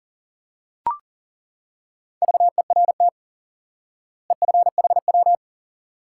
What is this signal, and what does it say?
Morse code, heard as a keyed tone, about 40 wpm, 700 Hz. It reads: VERT EFHW
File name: IQE Course 3 - Lesson 89 - YAGI - Two Elements - 40wpm